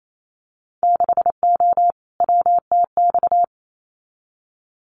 Morse code, heard as a keyed tone, about 28 wpm, 700 Hz. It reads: 6O WTX
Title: Winter Field Day Exchanges Encore 28wpm